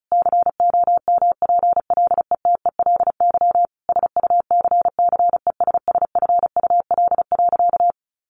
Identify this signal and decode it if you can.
Morse code, heard as a keyed tone, about 34 wpm, 700 Hz. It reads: COMPLETELY SUCCESSFUL.
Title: Sentences from Top 1100 Words 35wpm